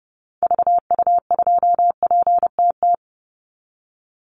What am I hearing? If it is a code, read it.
Morse code, heard as a keyed tone, about 30 wpm, 700 Hz. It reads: VU2PTT
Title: Call Signs Repeated 2x - 30wpm